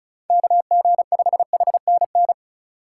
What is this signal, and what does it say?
Morse code, heard as a keyed tone, about 35 wpm, 700 Hz. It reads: KG5HNN
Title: USA Call Signs - 6 Characters - 3x Character Spacing - 35wpm